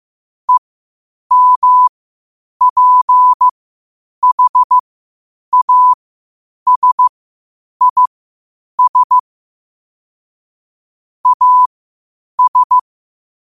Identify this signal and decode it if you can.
Morse code, heard as a keyed tone, about 15 wpm, spaced as for 8 wpm, 1000 Hz. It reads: EMPHASIS AS
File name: qotdfiles / 8wpm